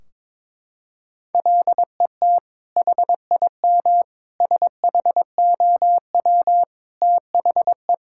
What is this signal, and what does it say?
Morse code, heard as a keyed tone, about 22 wpm, 700 Hz. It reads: LET HIM SHOW THE
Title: Sentences from Top 1600 Words 22wpm